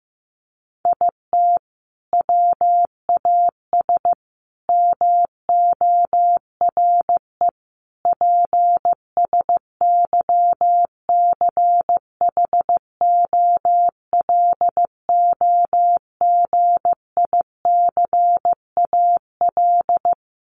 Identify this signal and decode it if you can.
Morse code, heard as a keyed tone, about 15 wpm, 700 Hz. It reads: IT WAS MORE PSYCHOLOGICAL
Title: Sentences from Top 1400 Words 15wpm